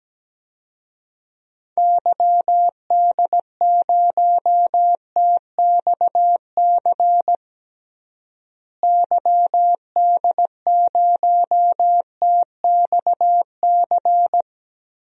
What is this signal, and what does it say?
Morse code, heard as a keyed tone, about 17 wpm, 700 Hz. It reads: YD0TXC YD0TXC
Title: DXCC-100 Call Signs - 6 Characters - Repeated 3x - 17wpm